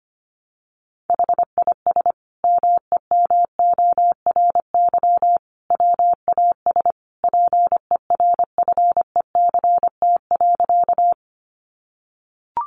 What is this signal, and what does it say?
Morse code, heard as a keyed tone, about 25 wpm, 700 Hz. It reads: HIS MEMORY WAS PERFECT.
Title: Sentences from Top 1300 Words 25wpm